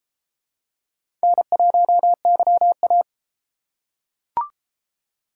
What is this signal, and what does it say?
Morse code, heard as a keyed tone, about 33 wpm, 700 Hz. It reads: N1YA